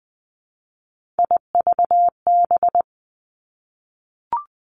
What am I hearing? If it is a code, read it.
Morse code, heard as a keyed tone, about 20 wpm, 700 Hz. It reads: IVB
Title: Three Challenging Letter-Number Combinations 20wpm